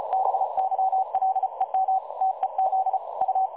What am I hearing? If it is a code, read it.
Morse code, heard as a keyed tone, about 33 wpm, 760 Hz. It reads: BYLEM TEZ U
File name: dlazaawansowanych / nagraniazpasma / websdr recording 3565.0kHz